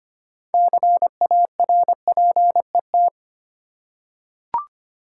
Morse code, heard as a keyed tone, about 25 wpm, 700 Hz. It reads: CARPET
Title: Six Letter Words - 25wpm